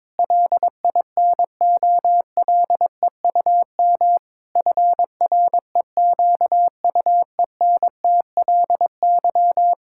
Morse code, heard as a keyed tone, about 22 wpm, 700 Hz. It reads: LINOLEUM FREQUENTLY
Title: Pangram Challenge v2 - 22wpm